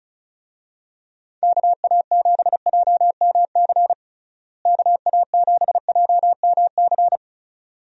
Morse code, heard as a keyed tone, about 36 wpm, 700 Hz. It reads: KA7JMC KA7JMC
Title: USA Call Signs - 6 Characters - Repeated 2x - 35wpm